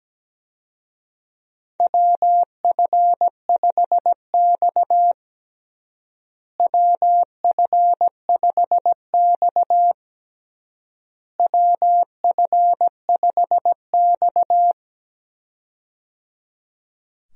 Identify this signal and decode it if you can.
Morse code, heard as a keyed tone, about 17 wpm, 700 Hz. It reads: WF5X WF5X WF5X
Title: USA Call Signs - 4 Characters - Repeated 3x - 17wpm